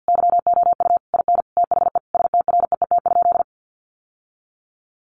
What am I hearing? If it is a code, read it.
Morse code, heard as a keyed tone, about 50 wpm, 700 Hz. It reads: YOU IN THE STREET?
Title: Sentences from Top 900 Words 50wpm